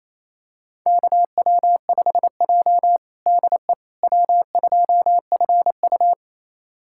Morse code, heard as a keyed tone, about 28 wpm, 700 Hz. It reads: KW5J DE W2FU
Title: USA Callsign DE Callsign - 4 Characters - 28wpm